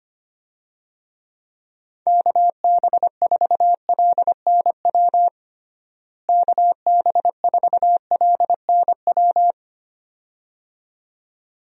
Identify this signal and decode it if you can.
Morse code, heard as a keyed tone, about 25 wpm, 700 Hz. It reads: KB4LNW KB4LNW
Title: USA Call Signs - 6 Characters - Repeated 2x - 25wpm